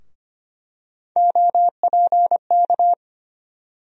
Morse code, heard as a keyed tone, about 25 wpm, 700 Hz. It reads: OPK
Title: USA Call Sign Postfixes - 3 Letters 25wpm